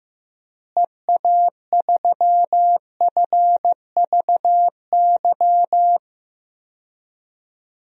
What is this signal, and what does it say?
Morse code, heard as a keyed tone, about 15 wpm, 700 Hz. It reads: EA3FVY
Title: DXCC-100 Call Signs - 6 Characters - Repeated 3x - 15wpm